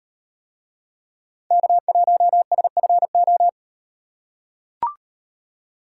Morse code, heard as a keyed tone, about 38 wpm, 700 Hz. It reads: K1SFO